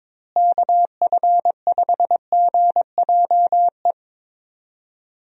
Morse code, heard as a keyed tone, about 21 wpm, 700 Hz. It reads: KF5GJE
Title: USA Call Signs - 6 Characters - Repeated 3x - 22wpm